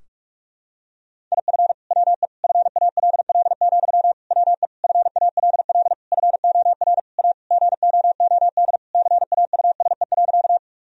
Morse code, heard as a keyed tone, about 46 wpm, 700 Hz. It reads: IF WE FALL, WE FALL FOR A GOOD CAUSE.
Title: Sentences from Top 1000 Words - Review Part 2 - 45wpm